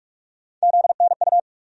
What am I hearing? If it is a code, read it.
Morse code, heard as a keyed tone, about 47 wpm, 700 Hz. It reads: ZNU
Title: USA Call Sign Postfixes - 3 Letters 45wpm